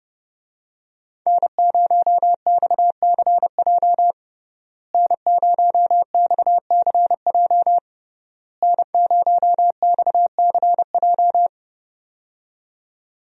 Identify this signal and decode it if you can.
Morse code, heard as a keyed tone, about 32 wpm, 700 Hz. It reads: N0XCJ N0XCJ N0XCJ
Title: USA Call Signs - 5 Characters - Repeated 3x - 30wpm